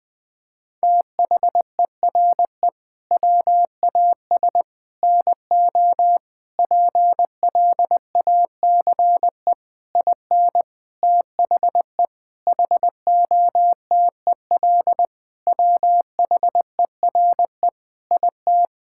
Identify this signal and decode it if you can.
Morse code, heard as a keyed tone, about 20 wpm, 700 Hz. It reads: THERE WAS NO PLACE IN THE HOTEL WHERE IT
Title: Sentences from Top 1600 Words 20wpm